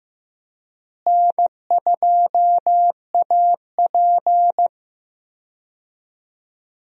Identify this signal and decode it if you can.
Morse code, heard as a keyed tone, about 15 wpm, 700 Hz. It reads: N2AP
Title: USA Call Signs - 4 Characters - Repeated 3x - 15wpm